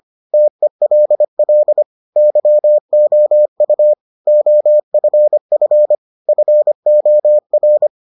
Morse code, heard as a keyed tone, about 25 wpm, 600 Hz. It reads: TELL YOU OFF FOR